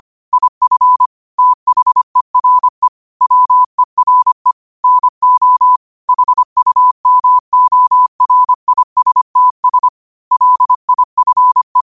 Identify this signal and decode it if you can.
Morse code, heard as a keyed tone, about 25 wpm, 1000 Hz. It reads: IF THERE WERE NO HUMORISTS LIFE
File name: qotdfiles / 25wpm